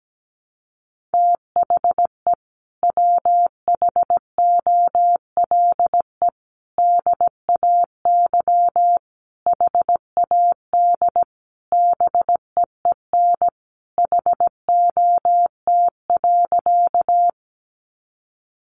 Morse code, heard as a keyed tone, about 17 wpm, 700 Hz. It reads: THE WHOLE DAY HAD BEEN HOT.
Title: Sentences from Top 500 Words 17wpm